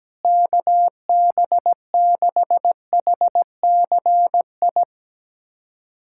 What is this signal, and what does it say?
Morse code, heard as a keyed tone, about 17 wpm, 700 Hz. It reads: KB6HCI
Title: USA Callsign DE Callsign 17wpm